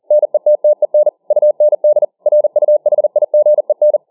Morse code, heard as a keyed tone, about 40 wpm, 600 Hz. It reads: NETTEN UND RUHIGEN